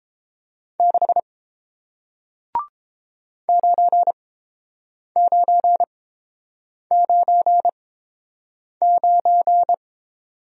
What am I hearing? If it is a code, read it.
Morse code, heard as a keyed tone, about 31 wpm, spaced as for 7 wpm, 700 Hz. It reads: 6 9999